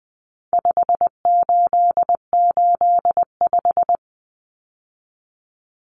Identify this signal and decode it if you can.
Morse code, heard as a keyed tone, about 20 wpm, 700 Hz. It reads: 5885
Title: Numbers - 4-Digit 20wpm